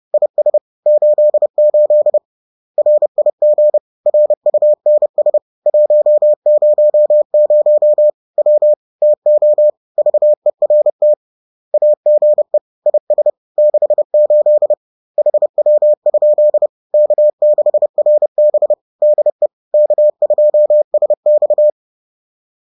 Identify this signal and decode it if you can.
Morse code, heard as a keyed tone, about 30 wpm, 600 Hz. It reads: IS 88 RIG RUNS 100 W TO VERT AGE IS 68 HW? K6RB DE K2SX
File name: 30wpm